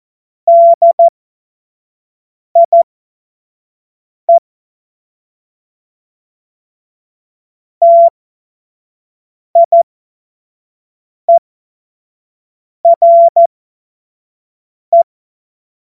Morse code, heard as a keyed tone, about 14 wpm, spaced as for 5 wpm, 680 Hz. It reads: DIE TIERE
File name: Morsekurs / Witze / Witz 18. April 2023 25 F70